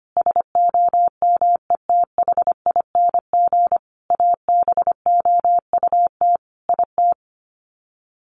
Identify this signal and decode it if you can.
Morse code, heard as a keyed tone, about 25 wpm, 700 Hz. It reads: SOMETHING ABOUT IT
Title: Sets of 5 Words 25wpm